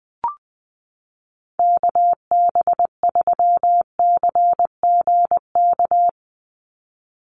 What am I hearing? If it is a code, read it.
Morse code, heard as a keyed tone, about 20 wpm, 700 Hz. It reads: KB3CGK
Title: USA Call Signs 20wpm